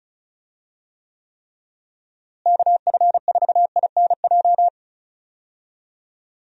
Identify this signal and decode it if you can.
Morse code, heard as a keyed tone, about 36 wpm, 700 Hz. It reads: KF4INJ
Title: USA Call Signs - 6 Characters - Repeated 3x - 35wpm